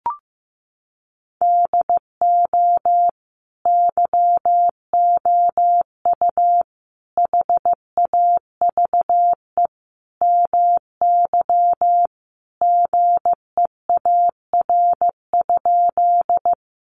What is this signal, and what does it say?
Morse code, heard as a keyed tone, about 15 wpm, 700 Hz. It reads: DO YOU HAVE MY GEAR?
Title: Sentences from Top 1400 Words 15wpm